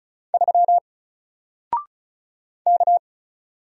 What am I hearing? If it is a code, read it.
Morse code, heard as a keyed tone, about 36 wpm, 700 Hz. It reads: 3 K
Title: Single Letter-Number Alt 35wpm